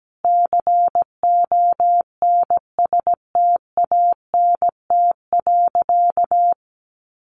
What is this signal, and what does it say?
Morse code, heard as a keyed tone, about 17 wpm, 700 Hz. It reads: CONSTANT.
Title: Sentences from Top 1500 Words 17wpm